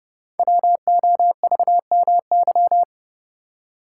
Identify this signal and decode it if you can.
Morse code, heard as a keyed tone, about 28 wpm, 700 Hz. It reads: WOVMY
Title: Code Groups - 5 Letters - 30wpm